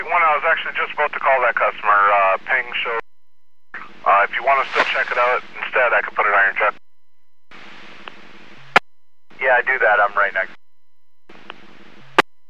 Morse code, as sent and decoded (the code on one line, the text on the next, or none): none